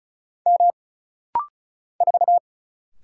Morse code, heard as a keyed tone, about 36 wpm, 700 Hz. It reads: M 4
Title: Single Letter-Number - Rapid-Fire 35wpm